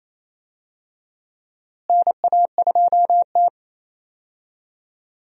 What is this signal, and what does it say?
Morse code, heard as a keyed tone, about 27 wpm, 700 Hz. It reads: NA2T
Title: USA Call Signs - 4 Characters - Repeated 2x - 28wpm